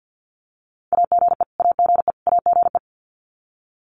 Morse code, heard as a keyed tone, about 50 wpm, 700 Hz. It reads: AGE AGE AGE
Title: Top 500 Words Repeated 50wpm